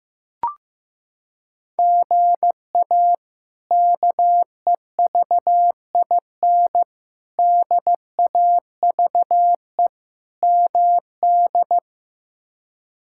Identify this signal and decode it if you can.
Morse code, heard as a keyed tone, about 15 wpm, 700 Hz. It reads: GA KEVIN DAVE MD
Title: K1USN SST QSO Practice - 15wpm